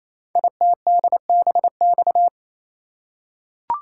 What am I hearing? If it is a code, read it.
Morse code, heard as a keyed tone, about 27 wpm, 700 Hz. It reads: ITDBX
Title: Code Groups - 5 Letters - 28wpm